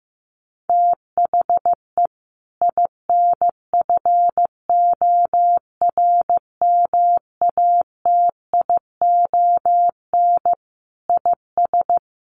Morse code, heard as a keyed tone, about 15 wpm, 700 Hz. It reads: THE INFORMATION IS
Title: Sentences from Top 1100 Words 15wpm